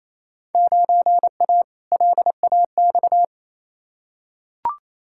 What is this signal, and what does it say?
Morse code, heard as a keyed tone, about 28 wpm, 700 Hz. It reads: 9A LAX
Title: Field Day Exchanges Encore - 28wpm